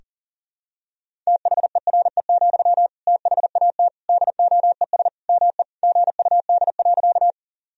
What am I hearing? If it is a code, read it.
Morse code, heard as a keyed tone, about 37 wpm, 700 Hz. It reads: THERE, THAT DOES ME GUD.
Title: Sentences from Top 200 Words and Abbreviations 40wpm